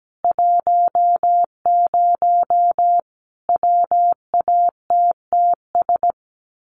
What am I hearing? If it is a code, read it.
Morse code, heard as a keyed tone, about 17 wpm, 700 Hz. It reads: 10 WATTS
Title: Common QSO Phrases 17wpm